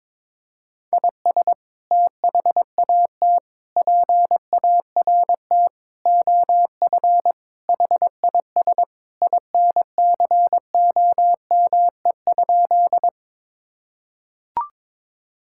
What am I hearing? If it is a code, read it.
Morse code, heard as a keyed tone, about 22 wpm, 700 Hz. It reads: IS THAT PART OF HIS INCOME?
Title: Sentences from Top 1000 Words - Review Part 5 - 22wpm